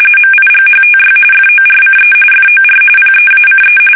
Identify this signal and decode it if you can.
RTTY, about 45.45 baud. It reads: MIGHTY FORCE FOR PEACE
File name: fsk850 / battleshipnj